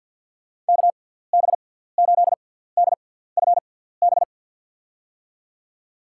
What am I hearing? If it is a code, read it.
Morse code, heard as a keyed tone, about 53 wpm, spaced as for 17 wpm, 700 Hz. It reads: KB8DFB
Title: USA Call Signs - 6 Characters - 6x Character Spacing - 50wpm